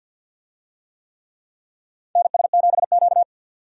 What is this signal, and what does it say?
Morse code, heard as a keyed tone, about 53 wpm, 700 Hz. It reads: NS7Q
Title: USA Call Signs - 4 Characters - 50wpm